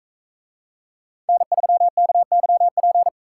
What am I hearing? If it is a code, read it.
Morse code, heard as a keyed tone, about 44 wpm, 700 Hz. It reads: N3KYP